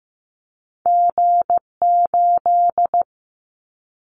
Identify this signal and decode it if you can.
Morse code, heard as a keyed tone, about 15 wpm, 700 Hz. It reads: G8
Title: Callsign Prefixes for Top 25 DXCC - 15wpm